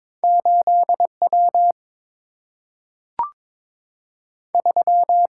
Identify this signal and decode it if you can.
Morse code, heard as a keyed tone, about 22 wpm, 700 Hz. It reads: 8 W 3
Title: Single-Letter Single-Number 22wpm